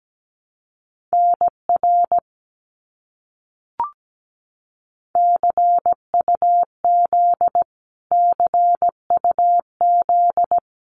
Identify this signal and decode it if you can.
Morse code, heard as a keyed tone, about 17 wpm, 700 Hz. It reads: NR CUZ CUZ
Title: Abbreviations Repeated 3x - 17wpm